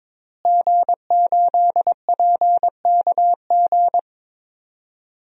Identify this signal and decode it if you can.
Morse code, heard as a keyed tone, about 22 wpm, 700 Hz. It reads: G8PKG
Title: DXCC-100 Call Signs - 5 Characters - Repeated 3x - 22wpm